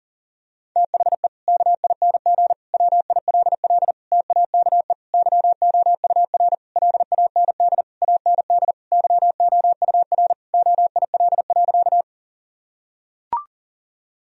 Morse code, heard as a keyed tone, about 40 wpm, 700 Hz. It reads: THE KING WILL TAKE YOUR LAND AND YOUR OIL.
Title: Sentences from Top 1000 Words - Review Part 4 - 40wpm